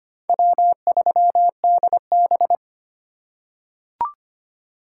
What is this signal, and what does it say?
Morse code, heard as a keyed tone, about 25 wpm, 700 Hz. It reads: W3DB
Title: USA Call Signs - 3x Character Spacing - 25wpm